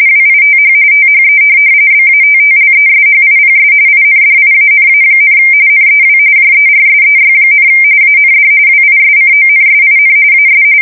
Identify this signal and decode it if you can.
RTTY, about 45.45 baud. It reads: WITH THE BATTLESHIPS INDIANA, NEW JERSEY, AND WISCONSIN AS
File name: fsk170 / USS-Massachusetts